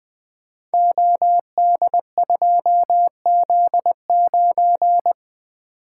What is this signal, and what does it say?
Morse code, heard as a keyed tone, about 20 wpm, 700 Hz. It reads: OD2Z9
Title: Code Groups - 5 Letter:Numbers - 20wpm